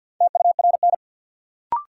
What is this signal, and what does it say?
Morse code, heard as a keyed tone, about 51 wpm, 700 Hz. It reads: TURN
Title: Sentences from Top 200 Words - New Word Drill - 50wpm